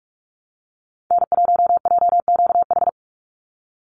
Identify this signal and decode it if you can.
Morse code, heard as a keyed tone, about 44 wpm, 700 Hz. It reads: N1JQH
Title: USA Call Signs 45wpm